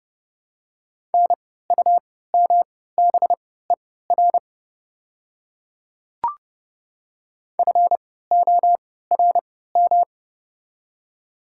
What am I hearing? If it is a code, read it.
Morse code, heard as a keyed tone, about 32 wpm, spaced as for 17 wpm, 700 Hz. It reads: NUMBER FORM
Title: 3x Character Spacing - Top 300 Words - 30wpm